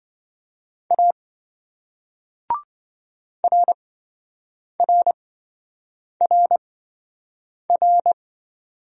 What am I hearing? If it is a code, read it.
Morse code, heard as a keyed tone, about 28 wpm, spaced as for 7 wpm, 700 Hz. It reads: A RRRR